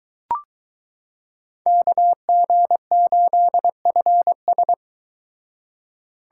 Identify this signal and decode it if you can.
Morse code, heard as a keyed tone, about 22 wpm, 700 Hz. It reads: KG8FS